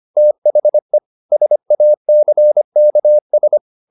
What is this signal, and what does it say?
Morse code, heard as a keyed tone, about 25 wpm, 600 Hz. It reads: THE SACKS
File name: the old man and the sea part 1 25wpm